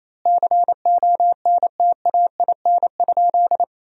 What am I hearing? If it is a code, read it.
Morse code, heard as a keyed tone, about 27 wpm, 700 Hz. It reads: CONTAIN?
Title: 2x Word-Spacing - Sentences from Top 500 Words - 28wpm